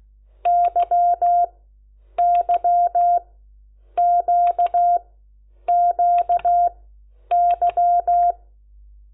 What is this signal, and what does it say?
Morse code, heard as a keyed tone, about 16 wpm, spaced as for 9 wpm, 700 Hz. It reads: YYQQY
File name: Lecciones / CLASE Nº10